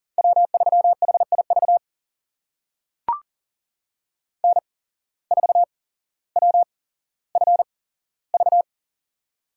Morse code, heard as a keyed tone, about 38 wpm, spaced as for 17 wpm, 700 Hz. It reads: W3HIV N4WFV
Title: USA Call Signs - 5 Characters - 8x Character Spacing - 40wpm